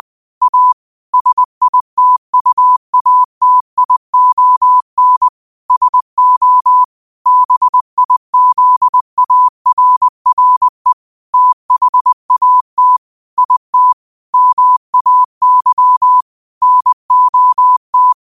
Morse code, heard as a keyed tone, about 20 wpm, 1000 Hz. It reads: A SITUATION SO BIZARRE THAT IT MAY NOT
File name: 20wpm